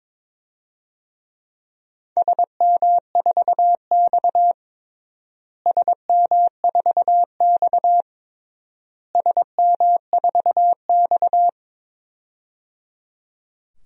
Morse code, heard as a keyed tone, about 22 wpm, 700 Hz. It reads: SM4X SM4X SM4X
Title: DXCC-100 Call Signs - 4 Characters - Repeated 3x - 22wpm